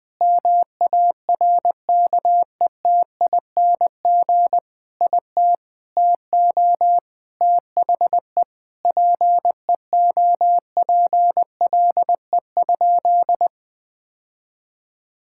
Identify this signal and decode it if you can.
Morse code, heard as a keyed tone, about 21 wpm, 700 Hz. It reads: MARKETING IT TO THE PEOPLE?
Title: Sentences from Top 1000 Words - Review Part 5 - 20wpm